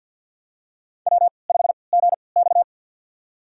Morse code, heard as a keyed tone, about 51 wpm, spaced as for 29 wpm, 700 Hz. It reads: W5GX
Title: USA Call Signs - 4 Characters - 3x Character Spacing - 50wpm